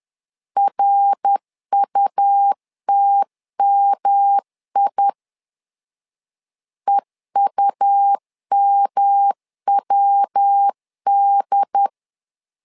Morse code, heard as a keyed tone, about 10 wpm, 800 Hz. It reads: RUTMI EUMWD
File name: LEZIONI / 3 MWRO / 3 MWR 10wpm 1952